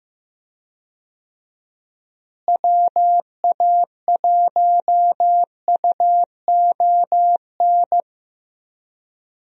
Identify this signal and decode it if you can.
Morse code, heard as a keyed tone, about 15 wpm, 700 Hz. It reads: WA1UON
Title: USA Call Signs - 6 Characters - Repeated 2x - 15wpm